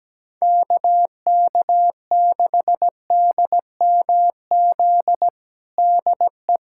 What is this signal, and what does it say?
Morse code, heard as a keyed tone, about 17 wpm, 700 Hz. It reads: KK6DMZ DE
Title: USA Callsign DE Callsign - 6 Characters - 17wpm